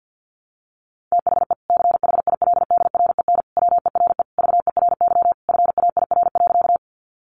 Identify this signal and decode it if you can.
Morse code, heard as a keyed tone, about 50 wpm, 700 Hz. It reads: THE CHILDREN WERE VERY FAIR.
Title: Sentences from Top 1300 Words 50wpm